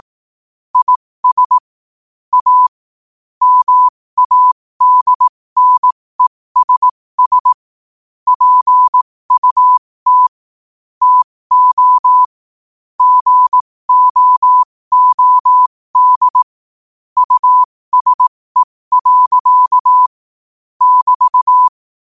Morse code, heard as a keyed tone, about 18 wpm, spaced as for 15 wpm, 1000 Hz. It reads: IS A MADNESS PUT TO GOOD USE. <BT>
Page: New quotes every day in morse code at 15 Words per minute.